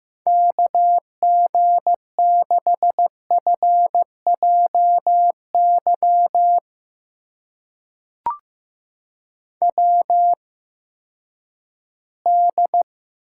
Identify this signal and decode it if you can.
Morse code, heard as a keyed tone, about 15 wpm, spaced as for 10 wpm, 700 Hz. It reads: KG6FJY WD
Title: USA Call Signs - 6 Characters - 8x Character Spacing - 15wpm